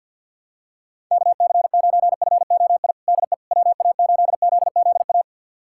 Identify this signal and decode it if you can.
Morse code, heard as a keyed tone, about 52 wpm, 700 Hz. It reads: KC9FOI DE WA8ZZA
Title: USA Callsign DE Callsign - 6 Characters - 50wpm